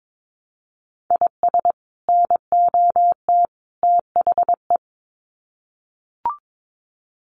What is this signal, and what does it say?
Morse code, heard as a keyed tone, about 22 wpm, 700 Hz. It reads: IS NOT THE
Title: Sets of 3 Words 22wpm